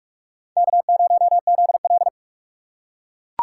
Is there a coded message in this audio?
Yes: Morse code from 0.6 to 2.1 s, heard as a keyed tone, about 38 wpm, 700 Hz. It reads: K0ZL